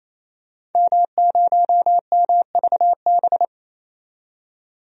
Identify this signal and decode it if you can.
Morse code, heard as a keyed tone, about 28 wpm, 700 Hz. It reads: M0MVB
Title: DXCC-100 Call Signs - 5 Characters - 28wpm